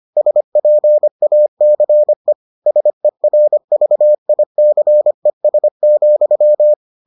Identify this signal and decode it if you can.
Morse code, heard as a keyed tone, about 25 wpm, 600 Hz. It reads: SPACE SERVICES,